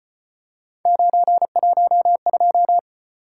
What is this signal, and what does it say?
Morse code, heard as a keyed tone, about 35 wpm, 700 Hz. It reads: 912